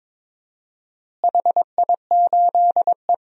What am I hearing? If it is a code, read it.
Morse code, heard as a keyed tone, about 22 wpm, 700 Hz. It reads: HI8E
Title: DXCC-100 Call Signs - 4 Characters - Repeated 3x - 22wpm